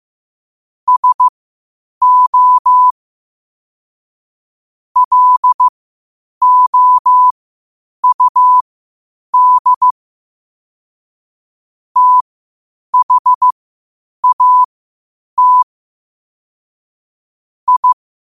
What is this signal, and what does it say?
Morse code, heard as a keyed tone, about 15 wpm, spaced as for 8 wpm, 1000 Hz. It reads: SO LOUD THAT I
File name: qotdfiles / 8wpm